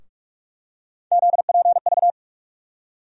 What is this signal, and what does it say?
Morse code, heard as a keyed tone, about 48 wpm, 700 Hz. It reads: ZPV
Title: USA Call Sign Postfixes - 3 Letters 45wpm